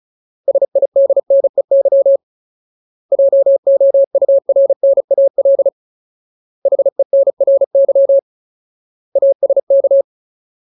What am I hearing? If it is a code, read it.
Morse code, heard as a keyed tone, about 34 wpm, 550 Hz. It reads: SIDNEY JOURNAL HENRY ASK
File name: Ultimate Word List / 35WPM Ultimate